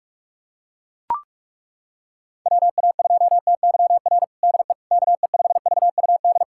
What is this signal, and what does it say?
Morse code, heard as a keyed tone, about 41 wpm, 700 Hz. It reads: WA2TYR DE KE5VUD
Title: USA Callsign DE Callsign - 6 Characters - 45wpm